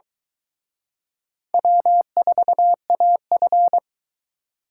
Morse code, heard as a keyed tone, about 23 wpm, 700 Hz. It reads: W4AF